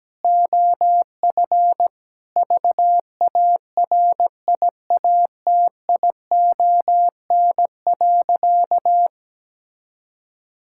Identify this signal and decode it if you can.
Morse code, heard as a keyed tone, about 17 wpm, 700 Hz. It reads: OF VARIATION.
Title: Sentences from Top 1600 Words 17wpm